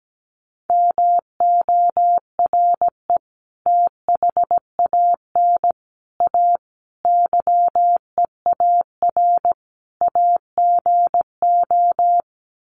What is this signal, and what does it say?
Morse code, heard as a keyed tone, about 17 wpm, 700 Hz. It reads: MORE THAN A YEAR AGO
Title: Sets of 5 Words Encore2 17wpm